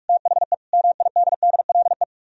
Morse code, heard as a keyed tone, about 44 wpm, 700 Hz. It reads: THE MIDDLE